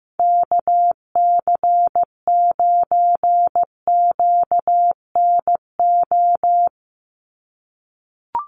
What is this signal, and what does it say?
Morse code, heard as a keyed tone, about 15 wpm, 700 Hz. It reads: KC9QNO
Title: USA Call Signs 15wpm